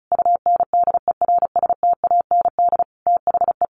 Morse code, heard as a keyed tone, about 34 wpm, 700 Hz. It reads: UNDERSTAND THE
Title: Sentences from Top 1200 Words 35wpm